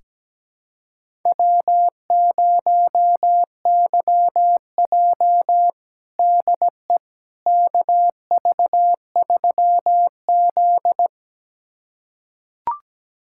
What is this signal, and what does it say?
Morse code, heard as a keyed tone, about 17 wpm, 700 Hz. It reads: W0YJ DE KV3Z
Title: USA Callsign DE Callsign - 4 Characters - 17wpm